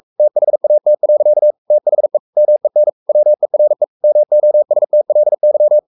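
Morse code, heard as a keyed tone, about 44 wpm, 600 Hz. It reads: THAT. THE MEN WERE MOSTLY
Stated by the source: Morse code at 43 wpm sending Ken Follett's novel Never in English, from CW med Gnister.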